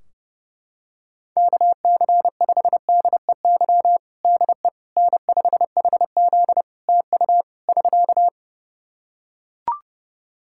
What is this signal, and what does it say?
Morse code, heard as a keyed tone, about 30 wpm, 700 Hz. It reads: KC5DEY DE N5HZ TU <SK>
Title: Practice QSO - Exchange 4 - Ending - 30wpm